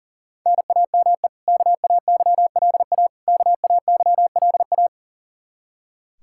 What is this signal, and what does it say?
Morse code, heard as a keyed tone, about 40 wpm, 700 Hz. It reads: NAME KAYLA KAYLA
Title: Practice QSO - Exchange 1 - Beginning - 40wpm